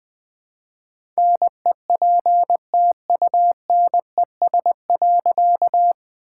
Morse code, heard as a keyed tone, about 20 wpm, 700 Hz. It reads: NEPTUNES.
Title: All-Age Appropriate Jokes - 20wpm